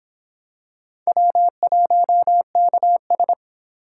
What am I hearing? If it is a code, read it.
Morse code, heard as a keyed tone, about 26 wpm, 700 Hz. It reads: W1KS